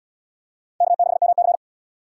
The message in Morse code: K6MC

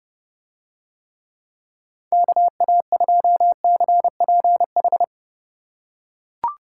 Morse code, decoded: KA2CPH

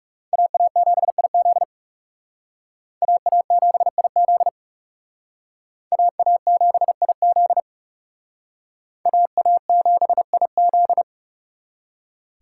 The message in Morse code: AA7IZ AA7IZ AA7IZ AA7IZ

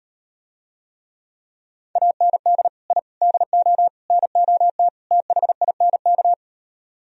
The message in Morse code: AND I DO NOT THINK

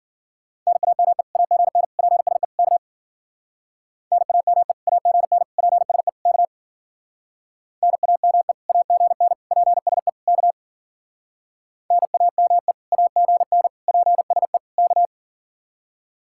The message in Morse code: NAME AGN PSE K NAME AGN PSE K NAME AGN PSE K NAME AGN PSE K

